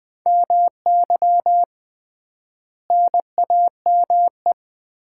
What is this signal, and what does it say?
Morse code, heard as a keyed tone, about 20 wpm, 700 Hz. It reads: MY NAME
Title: Common QSO Phrases - 3x Word-Spacing - 20wpm